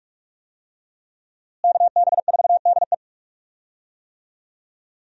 Morse code, heard as a keyed tone, about 43 wpm, 700 Hz. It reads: KB4DE
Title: USA Call Signs - 5 Characters - Repeated 3x - 45wpm